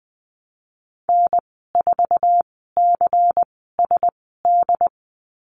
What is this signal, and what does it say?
Morse code, heard as a keyed tone, about 19 wpm, spaced as for 14 wpm, 700 Hz. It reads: N4CSD
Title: 128 Lesson 21g - Number 6 - Call Signs 20wpm